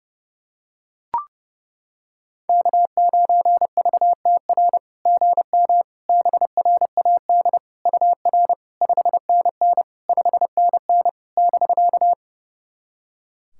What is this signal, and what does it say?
Morse code, heard as a keyed tone, about 28 wpm, 690 Hz. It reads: K9VTR GM BRAD UR 5NN 5NN <BK>
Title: SOTA QSO Practice 30wpm